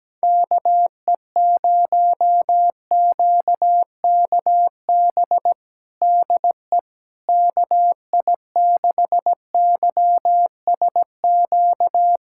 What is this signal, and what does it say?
Morse code, heard as a keyed tone, about 17 wpm, 700 Hz. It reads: KE0QKB DE KI6YSQ
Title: USA Callsign DE Callsign - 6 Characters - 17wpm